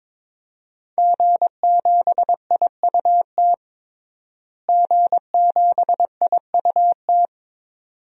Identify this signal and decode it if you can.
Morse code, heard as a keyed tone, about 22 wpm, 700 Hz. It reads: G7IUT G7IUT
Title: DXCC-100 Call Signs - 5 Characters - Repeated 2x - 22wpm